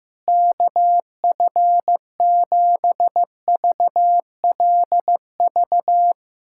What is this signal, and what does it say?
Morse code, heard as a keyed tone, about 15 wpm, 700 Hz. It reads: KF7VLV
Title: USA Callsign DE Callsign 15wpm